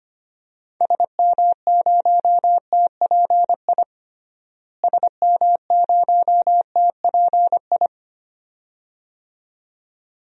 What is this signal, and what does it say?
Morse code, heard as a keyed tone, about 25 wpm, 700 Hz. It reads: SM0TPI SM0TPI
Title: DXCC-100 Call Signs - 6 Characters - Repeated 2x - 25wpm